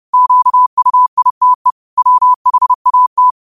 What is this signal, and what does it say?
Morse code, heard as a keyed tone, about 30 wpm, 1000 Hz. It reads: QUITE WHAT
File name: qotdfiles / 30wpm